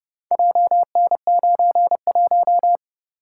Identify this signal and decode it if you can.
Morse code, heard as a keyed tone, about 30 wpm, 700 Hz. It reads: JN91